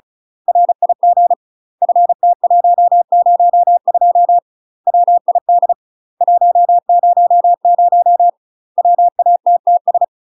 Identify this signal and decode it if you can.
Morse code, heard as a keyed tone, about 34 wpm, 700 Hz. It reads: RIG FT102 WID 100 WATTS
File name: UK MASTER QSOs TRAINING/35WPM